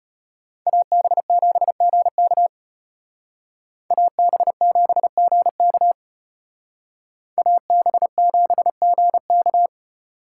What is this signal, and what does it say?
Morse code, heard as a keyed tone, about 32 wpm, 700 Hz. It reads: AB7GK AB7GK AB7GK